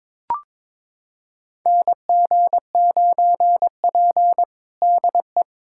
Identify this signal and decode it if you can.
Morse code, heard as a keyed tone, about 22 wpm, 700 Hz. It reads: NG9P DE
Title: USA Callsign DE Callsign - 4 Characters - 22wpm